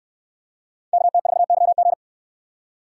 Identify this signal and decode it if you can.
Morse code, heard as a keyed tone, about 65 wpm, 700 Hz. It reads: KT4YG